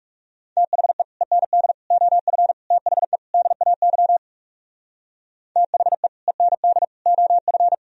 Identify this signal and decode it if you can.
Morse code, heard as a keyed tone, about 34 wpm, 700 Hz. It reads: THE END OF THE DAY THE END OF
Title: Speed-Racing - Sets of 5 Words 30wpm